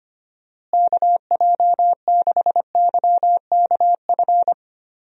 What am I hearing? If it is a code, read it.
Morse code, heard as a keyed tone, about 25 wpm, 700 Hz. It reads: KJ6YKF
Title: USA Call Signs - 6 Characters - Repeated 2x - 25wpm